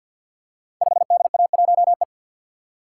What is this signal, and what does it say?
Morse code, heard as a keyed tone, about 53 wpm, 700 Hz. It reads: 5DA1E
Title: Code Groups - 5 Letter:Numbers - 50wpm